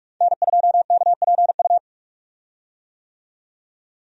Morse code, heard as a keyed tone, about 43 wpm, 700 Hz. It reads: N2KPU
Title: USA Call Signs - 5 Characters - 45wpm